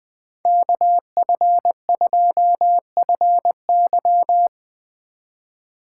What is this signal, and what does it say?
Morse code, heard as a keyed tone, about 20 wpm, 700 Hz. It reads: KF2FY